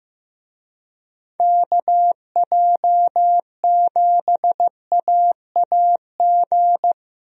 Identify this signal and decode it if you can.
Morse code, heard as a keyed tone, about 15 wpm, 700 Hz. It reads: KJ7AAG
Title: USA Call Signs - 6 Characters - Repeated 2x - 15wpm